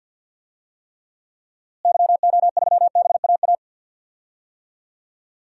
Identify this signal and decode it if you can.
Morse code, heard as a keyed tone, about 47 wpm, 700 Hz. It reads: YO3BAA
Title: DXCC-100 Call Signs - 6 Characters - Repeated 3x - 50wpm